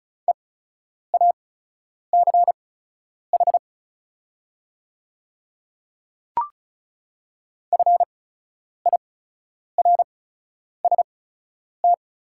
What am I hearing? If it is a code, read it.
Morse code, heard as a keyed tone, about 34 wpm, spaced as for 10 wpm, 700 Hz. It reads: EACH FIRST
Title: 8x Character Spacing - Top 100 Words - 35wpm